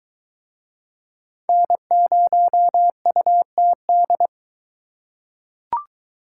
Morse code, heard as a keyed tone, about 22 wpm, 700 Hz. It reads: N0UTD